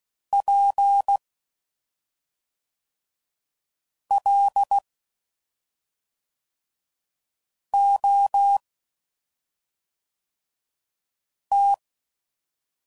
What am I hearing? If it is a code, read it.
Morse code, heard as a keyed tone, about 16 wpm, 800 Hz. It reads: P L O T
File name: Suffixes / Special ING's / SlowToFaster ING's